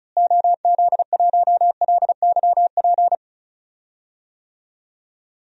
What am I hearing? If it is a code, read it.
Morse code, heard as a keyed tone, about 36 wpm, 700 Hz. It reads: OZ1LYP
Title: DXCC-100 Call Signs - 6 Characters - Repeated 3x - 35wpm